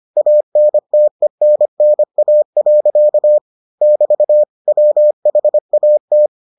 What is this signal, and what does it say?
Morse code, heard as a keyed tone, about 25 wpm, 600 Hz. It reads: ANTENNA. <BT> WHAT